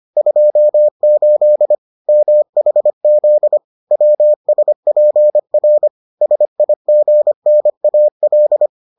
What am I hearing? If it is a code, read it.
Morse code, heard as a keyed tone, about 25 wpm, 600 Hz. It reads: 28 MHZ WSPR SIGNAL